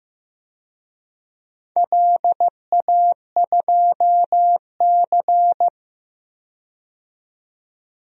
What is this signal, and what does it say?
Morse code, heard as a keyed tone, about 15 wpm, 700 Hz. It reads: LA2C
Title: DXCC-100 Call Signs - 4 Characters - 15wpm